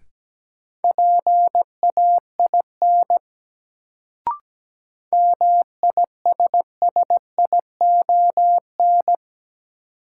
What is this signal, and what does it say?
Morse code, heard as a keyed tone, about 17 wpm, 700 Hz. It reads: PAIN MISSION
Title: Sentences from Top 1600 Words - New Word Drill - 17wpm